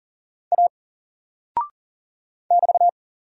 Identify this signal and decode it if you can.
Morse code, heard as a keyed tone, about 40 wpm, 700 Hz. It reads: A <BT>
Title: Single Character (Letter:Number:Punctuation:Prosign) Alt 40wpm